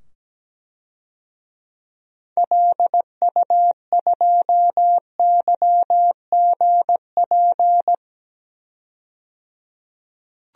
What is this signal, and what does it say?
Morse code, heard as a keyed tone, about 17 wpm, 700 Hz. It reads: LU2YGP
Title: DXCC-100 Call Signs - 6 Characters - 17wpm